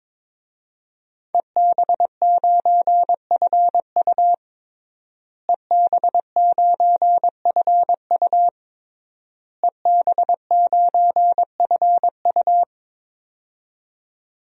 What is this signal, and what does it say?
Morse code, heard as a keyed tone, about 22 wpm, 700 Hz. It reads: EB9FU EB9FU EB9FU
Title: DXCC-100 Call Signs - 5 Characters - Repeated 3x - 22wpm